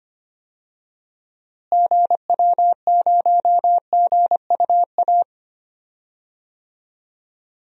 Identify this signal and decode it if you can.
Morse code, heard as a keyed tone, about 25 wpm, 700 Hz. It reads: GW0GUA
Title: DXCC-100 Call Signs - 6 Characters - Repeated 2x - 25wpm